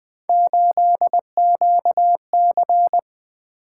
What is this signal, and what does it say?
Morse code, heard as a keyed tone, about 20 wpm, 700 Hz. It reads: 8QC
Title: Three Letter-Number Combinations - 20wpm